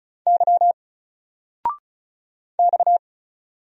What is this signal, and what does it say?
Morse code, heard as a keyed tone, about 36 wpm, 700 Hz. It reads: Y X
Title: Single Letter-Number 35wpm